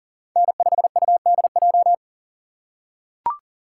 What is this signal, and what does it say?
Morse code, heard as a keyed tone, about 40 wpm, 700 Hz. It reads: N5UDJ
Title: USA Call Signs - 5 Characters - 3x Character Spacing - 40wpm